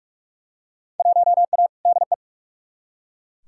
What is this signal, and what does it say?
Morse code, heard as a keyed tone, about 43 wpm, 700 Hz. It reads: 1A DE
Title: Field Day Exchanges Encore - 45wpm